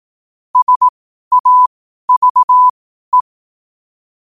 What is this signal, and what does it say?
Morse code, heard as a keyed tone, about 18 wpm, spaced as for 12 wpm, 1000 Hz. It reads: SAVE
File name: qotdfiles / 12wpm